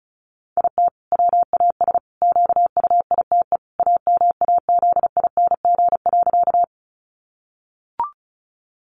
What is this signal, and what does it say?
Morse code, heard as a keyed tone, about 34 wpm, 700 Hz. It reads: IT WAS QUITE AMAZING.
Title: Sentences from Top 1500 Words 35wpm